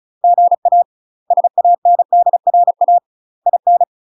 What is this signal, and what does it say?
Morse code, heard as a keyed tone, about 34 wpm, 700 Hz. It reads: GA SANDRA IN
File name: UK MASTER QSOs TRAINING / 35WPM / UK MASTER QSOs 0003